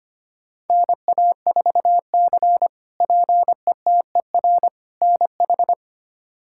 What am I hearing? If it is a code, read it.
Morse code, heard as a keyed tone, about 25 wpm, 700 Hz. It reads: NA4C PETER NH
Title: K1USN SST QSO Practice - 25wpm